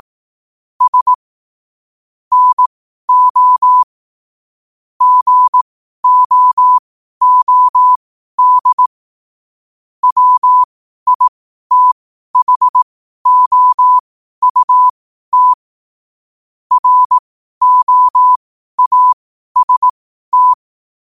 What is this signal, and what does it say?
Morse code, heard as a keyed tone, about 18 wpm, spaced as for 12 wpm, 1000 Hz. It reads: S NO GOOD WITHOUT ROAST
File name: qotdfiles / 12wpm